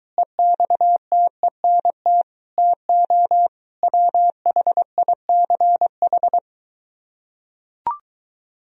Morse code, heard as a keyed tone, about 23 wpm, 700 Hz. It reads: EXTENT TO WHICH